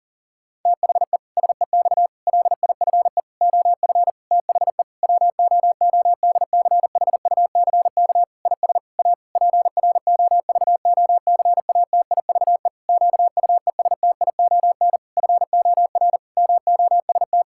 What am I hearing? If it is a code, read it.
Morse code, heard as a keyed tone, about 37 wpm, 700 Hz. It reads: THE SEX LIFE OF THE WOODCHUCK IS A PROVOCATIVE QUESTION FOR MOST
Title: Pangram Challenge v2 - 40wpm